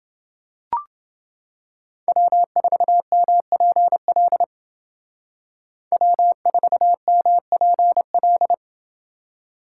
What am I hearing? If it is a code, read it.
Morse code, heard as a keyed tone, about 28 wpm, 700 Hz. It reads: W4MPL W4MPL